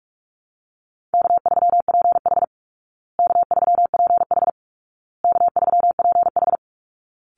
Morse code, heard as a keyed tone, about 44 wpm, 700 Hz. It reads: K3PH K3PH K3PH
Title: Easy Call Signs Repeated 3x - 45wpm